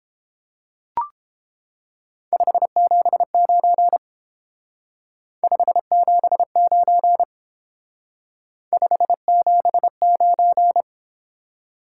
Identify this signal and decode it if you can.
Morse code, heard as a keyed tone, about 30 wpm, 700 Hz. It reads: 579 579 579